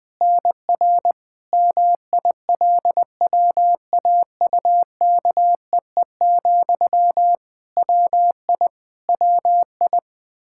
Morse code, heard as a keyed tone, about 21 wpm, 700 Hz. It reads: NR MILWAUKEE, WI WI
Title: Practice QSO - Exchange 1 - Beginning - 20wpm